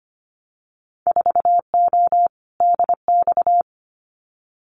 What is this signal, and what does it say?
Morse code, heard as a keyed tone, about 25 wpm, 700 Hz. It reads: 4O DX
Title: Winter Field Day Exchanges Encore 25wpm